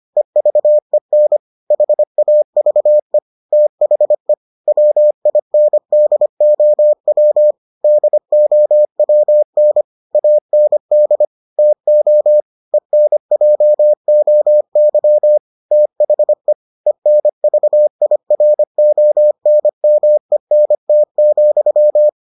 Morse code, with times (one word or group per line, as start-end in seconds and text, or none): 0.2-1.4 EVEN
1.7-3.2 HAVE
3.5-4.3 THE
4.7-7.5 WINDOW
7.8-9.8 DOWN
10.2-11.2 AND
11.6-12.4 TO
12.7-15.4 ENJOY
15.7-16.5 THE
16.9-22.1 ENVIRONMENT,